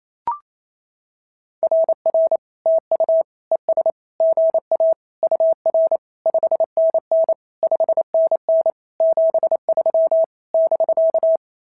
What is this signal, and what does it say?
Morse code, heard as a keyed tone, about 27 wpm, 650 Hz. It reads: RR TU ES GA UR 5NN 5NN 73 <BK>
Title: SOTA QSO Practice 28wpm